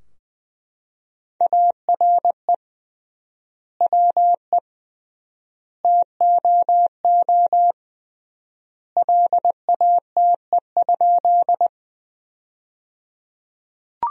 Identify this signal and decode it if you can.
Morse code, heard as a keyed tone, about 20 wpm, 700 Hz. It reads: ARE WE TOO LATE?
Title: 3x Word-Spacing - Sentences from Top 300 Words - 20wpm